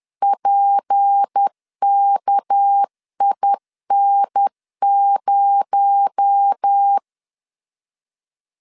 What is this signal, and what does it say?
Morse code, heard as a keyed tone, about 10 wpm, 800 Hz. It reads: PKIN0
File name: LEZIONI / 6 PKX0 / 6 PKX0 10wpm 2707